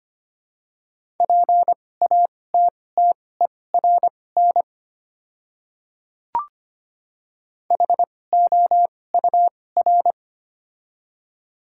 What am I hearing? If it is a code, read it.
Morse code, heard as a keyed tone, about 25 wpm, spaced as for 18 wpm, 700 Hz. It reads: PATTERN HOUR
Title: 2x Character Spacing - Top 500 Words - 25wpm